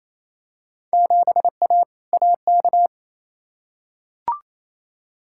Morse code, heard as a keyed tone, about 28 wpm, 700 Hz. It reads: 7A AK
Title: Field Day Exchanges Encore - 28wpm